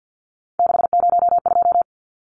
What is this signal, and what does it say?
Morse code, heard as a keyed tone, about 50 wpm, 700 Hz. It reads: 602
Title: Numbers - 3-Digit 50wpm